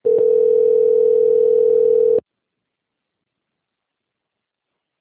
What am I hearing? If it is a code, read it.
Ringback tone (North America), 440 + 480 Hz.